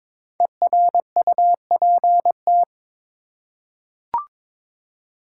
Morse code, heard as a keyed tone, about 23 wpm, 690 Hz. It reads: ERUPT
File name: 5 Letter Words - First 3 Letters Given - 22wpm